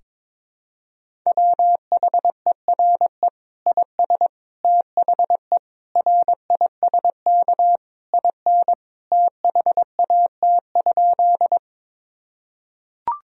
Morse code, heard as a keyed tone, about 22 wpm, 700 Hz. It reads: WHERE IS THE RISK IN THAT?
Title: Sentences from Top 1000 Words - Review Part 4 - 22wpm